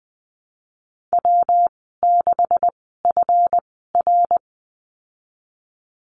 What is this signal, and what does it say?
Morse code, heard as a keyed tone, about 20 wpm, spaced as for 14 wpm, 700 Hz. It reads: W6FR
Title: 140 Lesson 23g - Letter f - Call Signs 20wpm